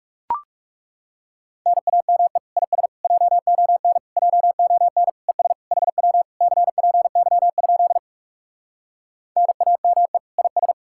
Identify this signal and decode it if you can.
Morse code, heard as a keyed tone, about 37 wpm, 700 Hz. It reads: NAME IS JON JON ES HW CPY? NAME IS
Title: Speed-Racing Common QSO Phrases Encore - 30wpm